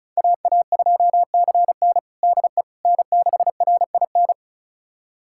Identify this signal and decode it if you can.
Morse code, heard as a keyed tone, about 34 wpm, 700 Hz. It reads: AA2CN DE N6RIN
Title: USA Callsign DE Callsign - 5 Characters - 35wpm